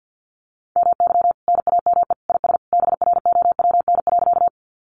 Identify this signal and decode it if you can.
Morse code, heard as a keyed tone, about 50 wpm, 700 Hz. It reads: MY NAME IS BROWN.
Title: Sentences from Top 1200 Words 50wpm